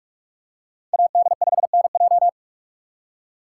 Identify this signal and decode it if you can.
Morse code, heard as a keyed tone, about 44 wpm, 700 Hz. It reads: AD5NJ